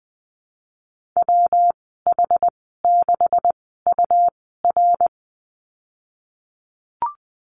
Morse code, heard as a keyed tone, about 19 wpm, spaced as for 14 wpm, 700 Hz. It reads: WH6UR
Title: 140 Lesson 23g - Letter f - Call Signs 20wpm